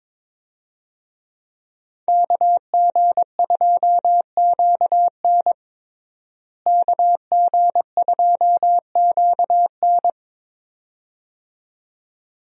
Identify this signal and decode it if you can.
Morse code, heard as a keyed tone, about 22 wpm, 700 Hz. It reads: KG2QN KG2QN
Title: USA Call Signs - 5 Characters - Repeated 2x - 22wpm